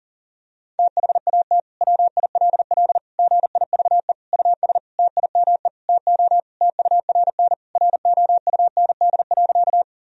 Morse code, heard as a keyed tone, about 40 wpm, 700 Hz. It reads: THAT WILL GIVE US TIME TO TURN ROUND.
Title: Sentences from Top 1000 Words - Review Part 1 - 40wpm